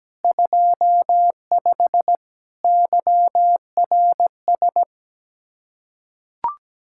Morse code, heard as a keyed tone, about 17 wpm, 700 Hz. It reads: 25 YRS
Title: K1USN SST QSO Practice 17wpm